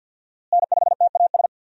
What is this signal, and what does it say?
Morse code, heard as a keyed tone, about 50 wpm, 700 Hz. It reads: N5TAS
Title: USA Call Signs - 5 Characters - Repeated 3x - 50wpm